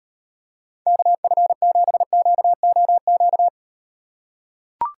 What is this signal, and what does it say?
Morse code, heard as a keyed tone, about 38 wpm, 700 Hz. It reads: KF7QOQ